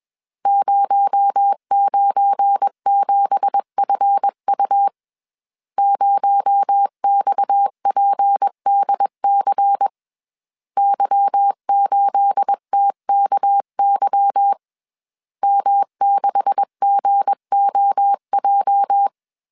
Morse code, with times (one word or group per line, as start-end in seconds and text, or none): 0.5-4.9 097FU
5.8-9.9 0XPDC
10.8-14.5 Y8TKY
15.4-19.1 M6GOJ